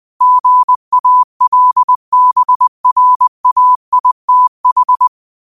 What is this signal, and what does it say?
Morse code, heard as a keyed tone, about 20 wpm, 1000 Hz. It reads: GALBRAITH
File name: qotdfiles / 20wpm